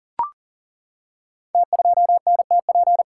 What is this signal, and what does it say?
Morse code, heard as a keyed tone, about 38 wpm, 700 Hz. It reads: T2NTP
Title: Code Groups - 5 Letter:Numbers - 40wpm